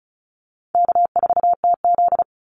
Morse code, heard as a keyed tone, about 34 wpm, 700 Hz. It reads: K4TZ
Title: Easy Call Signs Repeated 3x - 35wpm